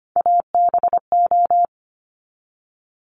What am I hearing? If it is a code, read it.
Morse code, heard as a keyed tone, about 25 wpm, 700 Hz. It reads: ABO
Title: Top Three Letter Combinations in Words 25wpm